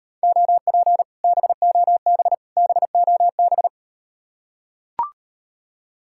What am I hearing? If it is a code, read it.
Morse code, heard as a keyed tone, about 38 wpm, 700 Hz. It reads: OP BOB BOB